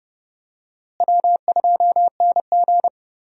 Morse code, heard as a keyed tone, about 31 wpm, 700 Hz. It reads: W2NG
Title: USA Call Signs - 4 Characters - Repeated 3x - 30wpm